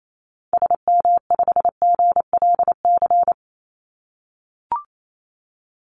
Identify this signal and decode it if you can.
Morse code, heard as a keyed tone, about 28 wpm, 700 Hz. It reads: SM5GLC
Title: Call Signs - 28wpm